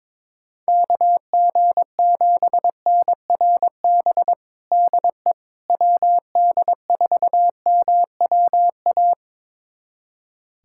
Morse code, heard as a keyed tone, about 22 wpm, 700 Hz. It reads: KG7NRB DE WD4MWA
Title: USA Callsign DE Callsign 22wpm